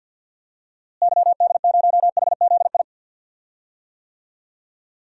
Morse code, heard as a keyed tone, about 52 wpm, 700 Hz. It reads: YD0HZI
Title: DXCC-100 Call Signs - 6 Characters - Repeated 3x - 50wpm